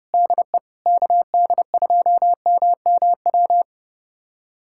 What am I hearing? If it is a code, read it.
Morse code, heard as a keyed tone, about 30 wpm, 700 Hz. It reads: DE KD2MMW
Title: USA Callsign DE Callsign 30wpm